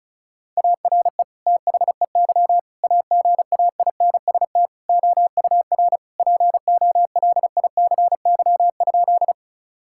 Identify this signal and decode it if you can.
Morse code, heard as a keyed tone, about 36 wpm, 700 Hz. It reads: ARE THEY AGAINST OUR POLICY?
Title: Sentences from Top 1000 Words - Review Part 5 - 35wpm